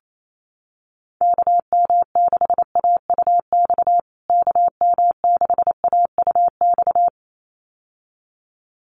Morse code, heard as a keyed tone, about 28 wpm, 700 Hz. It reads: KM6AUX KM6AUX
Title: USA Call Signs Repeated 2x 28wpm